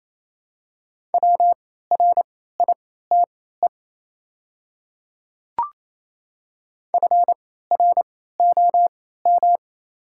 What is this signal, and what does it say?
Morse code, heard as a keyed tone, about 27 wpm, spaced as for 16 wpm, 700 Hz. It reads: WRITE FROM
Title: 3x Character Spacing - Top 100 Words - 28wpm